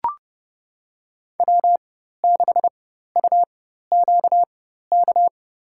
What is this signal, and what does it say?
Morse code, heard as a keyed tone, about 30 wpm, spaced as for 14 wpm, 700 Hz. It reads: W6UQK
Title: USA Call Signs - 5 Characters - 4x Character Spacing - 30wpm